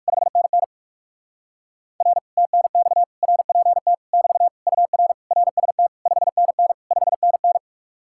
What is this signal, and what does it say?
Morse code, heard as a keyed tone, about 58 wpm, 700 Hz. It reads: 5NN R TNX RPT <BT> UR RST 5NN 5NN